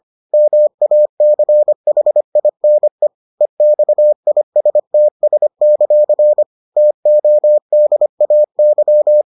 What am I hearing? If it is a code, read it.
Morse code, heard as a keyed tone, about 25 wpm, 600 Hz. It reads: MACHINE EXISTS; TODAY